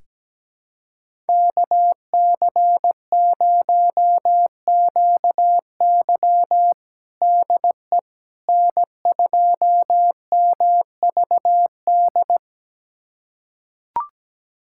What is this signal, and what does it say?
Morse code, heard as a keyed tone, about 17 wpm, 700 Hz. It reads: KC0QY DE N2MVD
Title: USA Callsign DE Callsign - 5 Characters - 17wpm